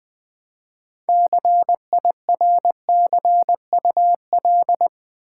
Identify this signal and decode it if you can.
Morse code, heard as a keyed tone, about 20 wpm, 700 Hz. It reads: CIRCUL
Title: Build A Word - 3 Letters At A Time - 9-Letter Words - 20wpm